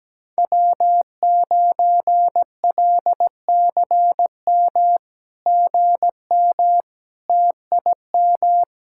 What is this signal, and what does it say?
Morse code, heard as a keyed tone, about 17 wpm, 700 Hz. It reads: W9LCM GM TIM
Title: SOTA QSO Practice 17wpm